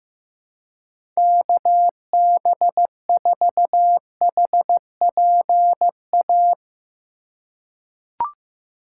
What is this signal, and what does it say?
Morse code, heard as a keyed tone, about 15 wpm, 700 Hz. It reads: KB4HPA
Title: USA Call Signs - 6 Characters - 2x Character Spacing - 15wpm